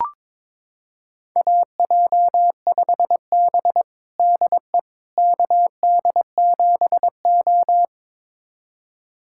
Morse code, heard as a keyed tone, about 22 wpm, 700 Hz. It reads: AJ5B DE KD7O
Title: USA Callsign DE Callsign - 4 Characters - 22wpm